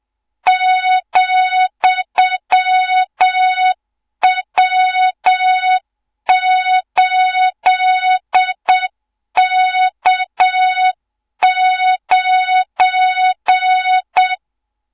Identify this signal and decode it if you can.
Morse code, heard as a keyed tone, about 7 wpm, 3700 Hz, with noise. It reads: ,W8K9
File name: Morze-HA2MN / hang / 24 GYAKORLAT